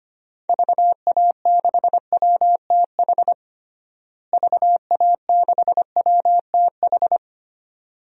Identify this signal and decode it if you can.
Morse code, heard as a keyed tone, about 25 wpm, 700 Hz. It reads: VA6WTH VA6WTH
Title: DXCC-100 Call Signs - 6 Characters - Repeated 2x - 25wpm